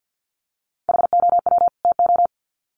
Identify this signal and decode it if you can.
Morse code, heard as a keyed tone, about 50 wpm, 700 Hz. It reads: HOW TO
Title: Sets of 2 Words 50wpm